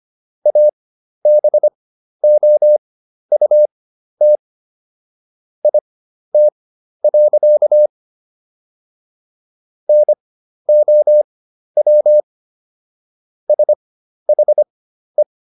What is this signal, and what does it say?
Morse code, heard as a keyed tone, about 25 wpm, spaced as for 12 wpm, 600 Hz. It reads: ABOUT IT. NOW SHE